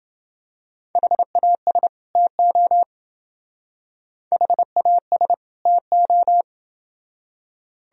Morse code, heard as a keyed tone, about 28 wpm, 700 Hz. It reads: HAS TO HAS TO